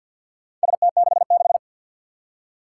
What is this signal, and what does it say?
Morse code, heard as a keyed tone, about 46 wpm, 700 Hz. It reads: ST66
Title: Four Challenging Letter-Number Combinations 50wpm